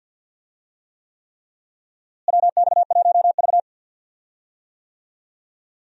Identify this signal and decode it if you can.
Morse code, heard as a keyed tone, about 53 wpm, 700 Hz. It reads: WX1V
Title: USA Call Signs - 4 Characters - Repeated 3x - 50wpm